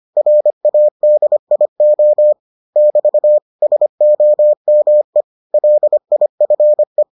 Morse code, heard as a keyed tone, about 25 wpm, 600 Hz. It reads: RADIO <BT> SOME LIFE